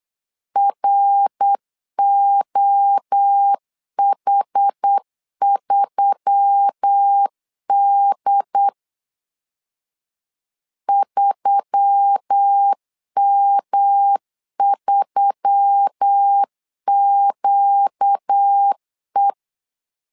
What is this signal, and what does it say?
Morse code, heard as a keyed tone, about 8 wpm, 800 Hz. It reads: ROH3D 3M3QE
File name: LEZIONI / 4 Q1S3 8wpm 3037